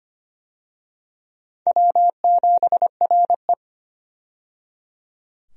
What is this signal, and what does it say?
Morse code, heard as a keyed tone, about 25 wpm, 700 Hz. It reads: W7RE
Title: USA Call Signs - 4 Characters - 25wpm